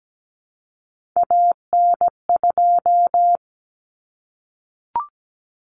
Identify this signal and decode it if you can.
Morse code, heard as a keyed tone, about 17 wpm, 700 Hz. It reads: AN2
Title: Common Cut Numbers - 3-Digit 17wpm